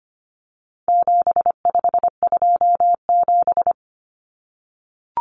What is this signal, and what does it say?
Morse code, heard as a keyed tone, about 25 wpm, 700 Hz. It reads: 7527
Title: Numbers - 4-Digit 25wpm